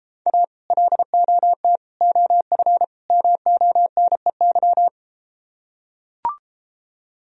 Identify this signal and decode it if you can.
Morse code, heard as a keyed tone, about 34 wpm, 700 Hz. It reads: A LOT OF MONEY